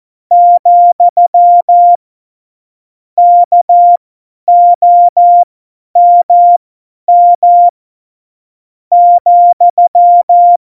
Morse code, heard as a keyed tone, about 14 wpm, spaced as for 8 wpm, 700 Hz. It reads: , KOMM,
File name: Morsekurs / Witze / Witz 05. Juni 2018, 50 F70